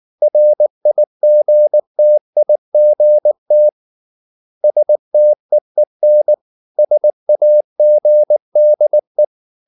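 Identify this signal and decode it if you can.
Morse code, heard as a keyed tone, about 19 wpm, 600 Hz. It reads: RIGTIGT STEEN SAGDE
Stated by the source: Slow (QRS) Morse practice at 18 wpm, CW med Gnister.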